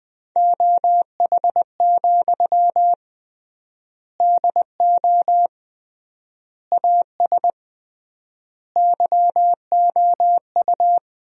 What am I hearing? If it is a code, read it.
Morse code, heard as a keyed tone, about 21 wpm, 700 Hz. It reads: OH, DO AS YOU
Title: 3x Word-Spacing - Sentences from Top 500 Words - 20wpm